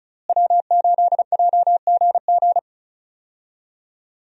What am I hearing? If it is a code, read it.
Morse code, heard as a keyed tone, about 34 wpm, 700 Hz. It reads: W8JGG
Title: USA Call Signs - 5 Characters - Repeated 3x - 35wpm